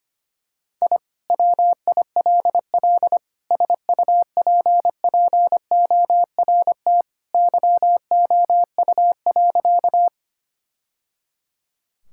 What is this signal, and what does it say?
Morse code, heard as a keyed tone, about 25 wpm, 700 Hz. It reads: I WILL SUPPORT YOU.
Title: Sentences from Top 1000 Words - Review Part 5 - 25wpm